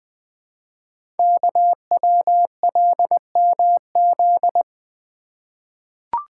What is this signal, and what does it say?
Morse code, heard as a keyed tone, about 20 wpm, 700 Hz. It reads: KWLMZ
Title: Code Groups - 5 Letter:Numbers - 20wpm